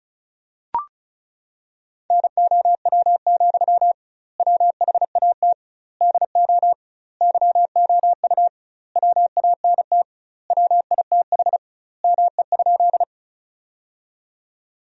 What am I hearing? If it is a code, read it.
Morse code, heard as a keyed tone, about 36 wpm, 700 Hz. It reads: NOW, WHAT DO YOU WANT WITH ME?
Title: 2x Word-Spacing - Sentences from Top 200 Words - 35wpm